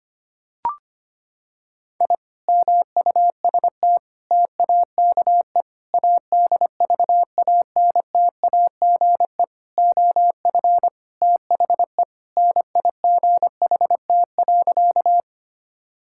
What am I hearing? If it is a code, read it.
Morse code, heard as a keyed tone, about 25 wpm, 700 Hz. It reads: I MUST TAKE ADVANTAGE OF THE NIGHT.
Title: Sentences from Top 1000 Words - Review Part 6 - 25wpm